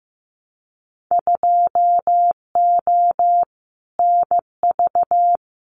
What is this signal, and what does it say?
Morse code, heard as a keyed tone, about 15 wpm, 700 Hz. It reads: 2O NV
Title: Winter Field Day Exchanges 15wpm